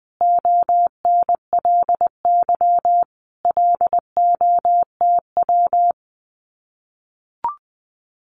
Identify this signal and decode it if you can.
Morse code, heard as a keyed tone, about 20 wpm, 700 Hz. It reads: ONLY LOTW
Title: Common QSO Phrases Encore 20wpm